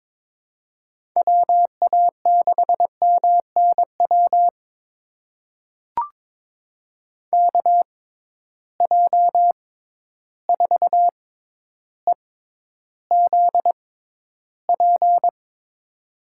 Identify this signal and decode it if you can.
Morse code, heard as a keyed tone, about 22 wpm, spaced as for 11 wpm, 700 Hz. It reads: WA6MNW KJ4EZP
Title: USA Call Signs - 6 Characters - 6x Character Spacing - 22wpm